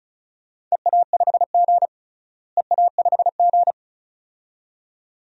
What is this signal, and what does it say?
Morse code, heard as a keyed tone, about 34 wpm, 700 Hz. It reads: EA5G EA5G
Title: DXCC-100 Call Signs - 4 Characters - Repeated 2x - 35wpm